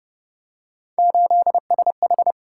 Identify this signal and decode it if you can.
Morse code, heard as a keyed tone, about 28 wpm, 700 Hz. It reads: 8SH